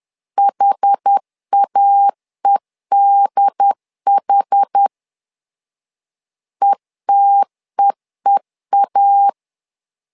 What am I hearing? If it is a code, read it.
Morse code, heard as a keyed tone, about 10 wpm, 800 Hz. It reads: HAEDH ETEEA
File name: LEZIONI / 2 UHDN / 2 UHD 10wpm 1826